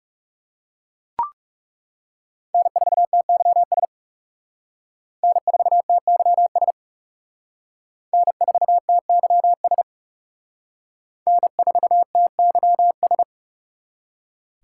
Morse code, heard as a keyed tone, about 34 wpm, 700 Hz. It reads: N4TYS N4TYS N4TYS N4TYS
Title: Speed Racing - USA 1x3 Call Signs 30wpm